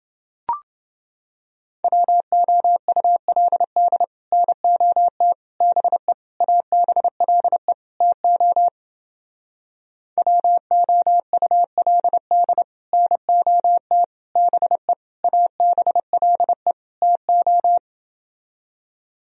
Morse code, heard as a keyed tone, about 30 wpm, 700 Hz. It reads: WOULD NOT BE ABLE TO WOULD NOT BE ABLE TO